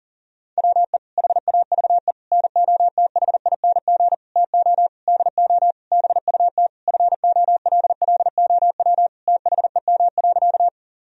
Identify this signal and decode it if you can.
Morse code, heard as a keyed tone, about 40 wpm, 700 Hz. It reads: WE HAVE NOTHING TO DO BUT FOLLOW THEM.
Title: Sentences from Top 1000 Words - Review Part 1 - 40wpm